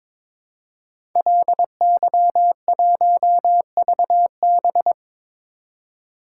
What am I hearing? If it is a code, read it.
Morse code, heard as a keyed tone, about 22 wpm, 700 Hz. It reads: LY1VB
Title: Code Groups - 5 Letter:Numbers - 22wpm